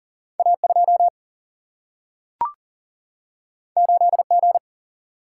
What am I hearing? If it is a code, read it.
Morse code, heard as a keyed tone, about 40 wpm, 700 Hz. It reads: A 2 8 G
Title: Single-Letter Single-Number 40wpm